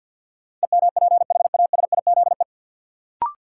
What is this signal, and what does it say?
Morse code, heard as a keyed tone, about 53 wpm, 700 Hz. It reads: EMPHASIZE
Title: Sentences from Top 1500 Words - New Word Drill - 50wpm